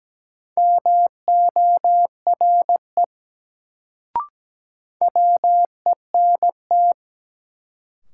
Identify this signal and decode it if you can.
Morse code, heard as a keyed tone, about 17 wpm, 700 Hz. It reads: MORE WENT
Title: Sets of 2 Words - New Word Drill - 17wpm